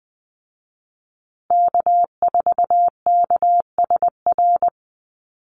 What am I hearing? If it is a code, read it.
Morse code, heard as a keyed tone, about 20 wpm, 700 Hz. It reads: K4KSR
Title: Call Signs Repeated 2x - 20wpm